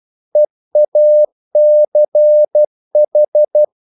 Morse code, heard as a keyed tone, about 12 wpm, 600 Hz. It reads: EACH
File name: mp3 files / the gardener - 12wpm